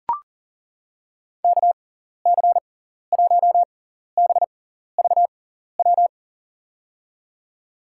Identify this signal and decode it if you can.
Morse code, heard as a keyed tone, about 40 wpm, spaced as for 14 wpm, 700 Hz. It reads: KC1BVW
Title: USA Call Signs - 6 Characters - 6x Character Spacing - 40wpm